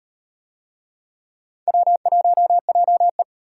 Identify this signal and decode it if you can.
Morse code, heard as a keyed tone, about 38 wpm, 700 Hz. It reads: W1JE